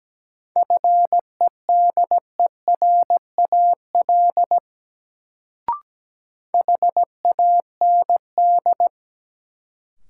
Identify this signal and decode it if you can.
Morse code, heard as a keyed tone, about 17 wpm, 700 Hz. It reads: FEDERAL HAND
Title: Sets of 3 Words - New Word Drill - 17wpm